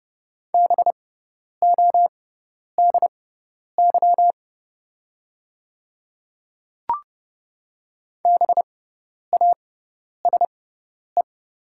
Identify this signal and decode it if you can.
Morse code, heard as a keyed tone, about 30 wpm, spaced as for 10 wpm, 700 Hz. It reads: BODY BASE
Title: 6x Character Spacing - Top 500 Words - 30wpm